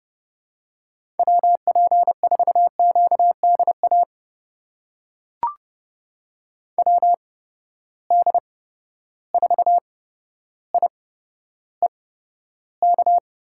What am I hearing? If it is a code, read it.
Morse code, heard as a keyed tone, about 30 wpm, spaced as for 13 wpm, 700 Hz. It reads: WP4QDA WD4IEK
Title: USA Call Signs - 6 Characters - 8x Character Spacing - 30wpm